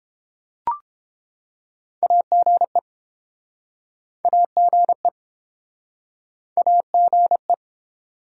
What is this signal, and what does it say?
Morse code, heard as a keyed tone, about 33 wpm, 700 Hz. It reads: AGE AGE AGE